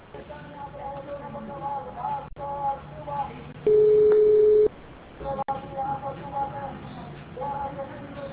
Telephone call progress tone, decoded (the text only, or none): ringback tone